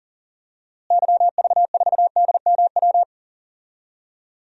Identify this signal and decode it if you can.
Morse code, heard as a keyed tone, about 40 wpm, 700 Hz. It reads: YV4DMW
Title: DXCC-100 Call Signs - 6 Characters - Repeated 2x - 40wpm